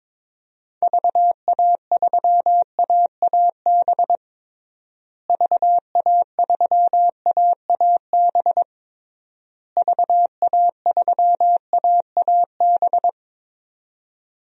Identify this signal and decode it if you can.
Morse code, heard as a keyed tone, about 21 wpm, 700 Hz. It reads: VA3AAB VA3AAB VA3AAB
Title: DXCC-100 Call Signs - 6 Characters - Repeated 3x - 22wpm